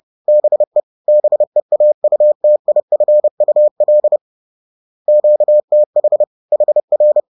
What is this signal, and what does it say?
Morse code, heard as a keyed tone, about 30 wpm, 600 Hz. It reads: BE BEAUTIFUL QTH HR